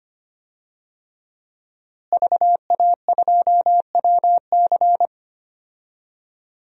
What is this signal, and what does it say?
Morse code, heard as a keyed tone, about 25 wpm, 700 Hz. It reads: VA2WC
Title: DXCC-100 Call Signs - 5 Characters - Repeated 3x - 25wpm